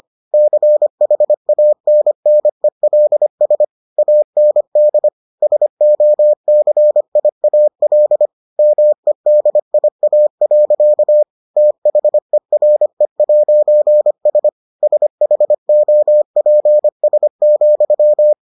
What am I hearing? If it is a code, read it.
Morse code, heard as a keyed tone, about 25 wpm, 600 Hz. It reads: CHANNELS AND SOCIAL MEDIA. THERE'S SHOPS,